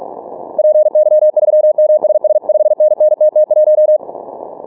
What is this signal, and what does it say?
Morse code, heard as a keyed tone, about 47 wpm, 610 Hz. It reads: PY3MSS 5NN TT1
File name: training / morserunner / Example